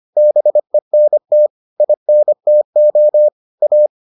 Morse code, heard as a keyed tone, about 25 wpm, 600 Hz. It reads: BENT INTO A